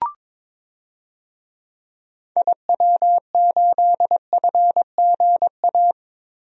Morse code, heard as a keyed tone, about 22 wpm, 700 Hz. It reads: IW8FGA
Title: DXCC-100 Call Signs - 6 Characters - Repeated 3x - 22wpm